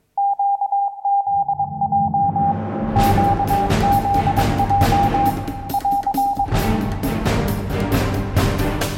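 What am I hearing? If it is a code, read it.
Morse code, heard as a keyed tone, about 22 wpm, 800 Hz. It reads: QZJ QZJ <AR>
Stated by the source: Morse code taught in a Swedish lesson on learning telegraphy, from QZJ and Utesändaren.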